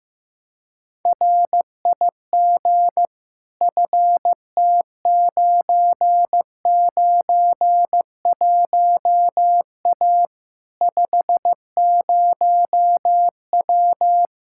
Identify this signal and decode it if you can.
Morse code, heard as a keyed tone, about 15 wpm, 700 Hz. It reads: RIG FT991A 50W
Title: Practice QSO - Exchange 2 - Info - 15wpm